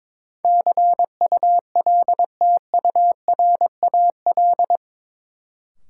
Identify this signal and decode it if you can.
Morse code, heard as a keyed tone, about 21 wpm, 700 Hz. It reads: CULTURAL
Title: Sentences from Top 1500 Words - New Word Drill - 22wpm